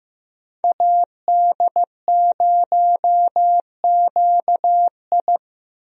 Morse code, heard as a keyed tone, about 15 wpm, 700 Hz. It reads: AD0QI
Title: USA Call Signs - 5 Characters - 15wpm